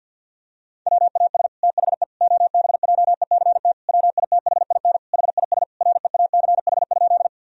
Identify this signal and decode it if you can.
Morse code, heard as a keyed tone, about 50 wpm, 700 Hz. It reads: WAS THE OBJECT WITHIN HIS REACH?
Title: Sentences from Top 1000 Words - Review Part 4 - 50wpm